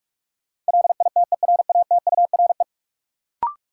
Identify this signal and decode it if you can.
Morse code, heard as a keyed tone, about 46 wpm, 700 Hz. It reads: LITERATURE
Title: Sentences from Top 600 Words - New Word Drill - 45wpm